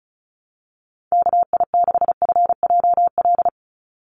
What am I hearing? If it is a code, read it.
Morse code, heard as a keyed tone, about 36 wpm, 700 Hz. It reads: KI6FJL
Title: USA Call Signs 35wpm